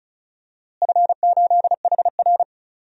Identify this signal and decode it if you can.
Morse code, heard as a keyed tone, about 36 wpm, 700 Hz. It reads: F8HR
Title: DXCC-100 Call Signs - 4 Characters - Repeated 3x - 35wpm